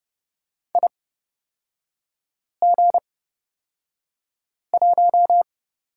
Morse code, heard as a keyed tone, about 31 wpm, spaced as for 5 wpm, 700 Hz. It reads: IG1
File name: Lesson 38e - Letter q - Three Characters 30wpm